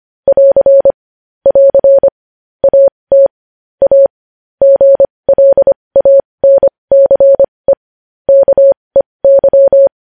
Morse code, heard as a keyed tone, about 25 wpm, spaced as for 20 wpm, 550 Hz. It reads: <AR> <AR> AT A GLANCE KEY
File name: podcast / News headlines-1x-25x20